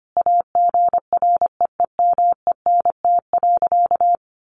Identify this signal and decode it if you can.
Morse code, heard as a keyed tone, about 25 wpm, 700 Hz. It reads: AGREEMENT.
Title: Sentences from Top 1500 Words 25wpm